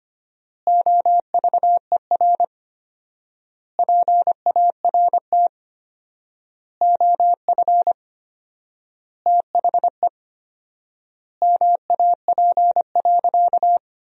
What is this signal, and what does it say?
Morse code, heard as a keyed tone, about 25 wpm, 700 Hz. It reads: OVER PART OF THE MAP.
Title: 4x Word-Spacing - Sentences from Top 400 Words - 25wpm